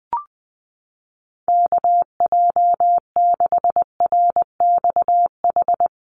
Morse code, heard as a keyed tone, about 20 wpm, 700 Hz. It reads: KJ6RXH
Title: USA Call Signs 20wpm